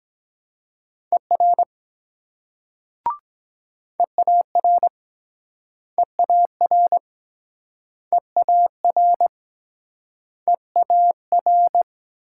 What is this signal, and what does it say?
Morse code, heard as a keyed tone, about 23 wpm, 700 Hz. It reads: ER EAR EAR EAR EAR